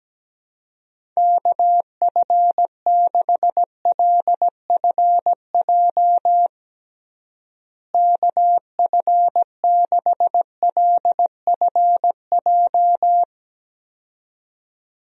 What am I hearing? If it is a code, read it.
Morse code, heard as a keyed tone, about 17 wpm, 700 Hz. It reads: KF6LFJ KF6LFJ
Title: USA Call Signs - 6 Characters - Repeated 2x - 17wpm